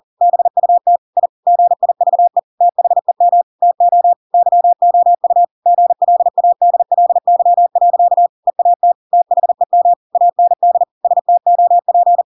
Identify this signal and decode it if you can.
Morse code, heard as a keyed tone, about 40 wpm, 700 Hz. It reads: BUT I GIVE THEM TO YOU GLADLY. EAT THEM AND STOP
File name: Pinocchio / Pinocchio ch7 40 40 ik1hge